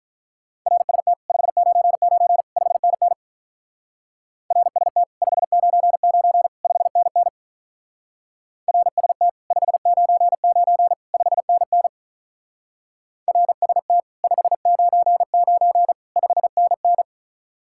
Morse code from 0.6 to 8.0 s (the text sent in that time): RST 599 5NN RST 599 5NN